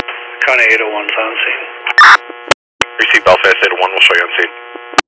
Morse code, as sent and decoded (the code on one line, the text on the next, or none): none